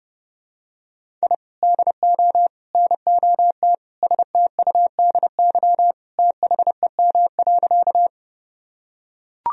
Morse code, heard as a keyed tone, about 32 wpm, 700 Hz. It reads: I DO NOT STUDY THEM.
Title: Sentences from Top 1000 Words - Review Part 1 - 30wpm